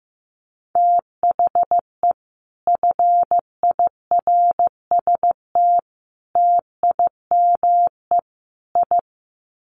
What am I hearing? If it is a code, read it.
Morse code, heard as a keyed tone, about 15 wpm, 700 Hz. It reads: THE FIRST TIME I
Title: Sets of 4 Words 15wpm